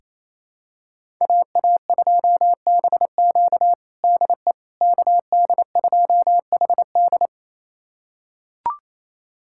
Morse code, heard as a keyed tone, about 28 wpm, 700 Hz. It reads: AA2BQ DE KD2HD
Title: USA Callsign DE Callsign - 5 Characters - 28wpm